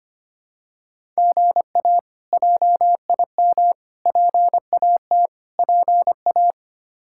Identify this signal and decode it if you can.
Morse code, heard as a keyed tone, about 25 wpm, 700 Hz. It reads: GA JIM PAT PA
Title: K1USN SST QSO Practice - 25wpm